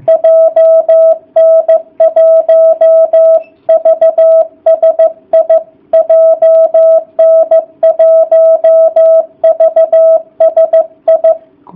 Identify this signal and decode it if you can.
Morse code, heard as a keyed tone, about 15 wpm, 660 Hz. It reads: JN1VSI JN1VSI